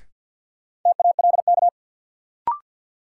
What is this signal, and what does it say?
Morse code, heard as a keyed tone, about 53 wpm, 700 Hz. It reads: TALK
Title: Sentences from Top 400 Words - New Word Drill - 50wpm